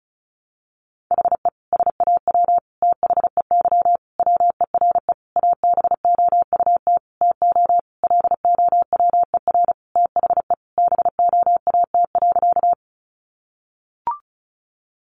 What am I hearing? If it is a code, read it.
Morse code, heard as a keyed tone, about 34 wpm, 700 Hz. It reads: HE SAW THEY WERE ABOUT TO LOWER THE BOAT.
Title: Sentences from Top 700 Words 35wpm